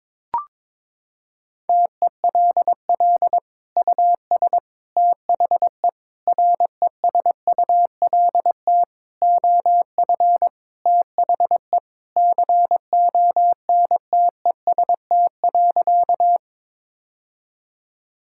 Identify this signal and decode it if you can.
Morse code, heard as a keyed tone, about 22 wpm, 700 Hz. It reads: TELL US THE RESULT OF THE CONTEST.
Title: Sentences from Top 1600 Words 22wpm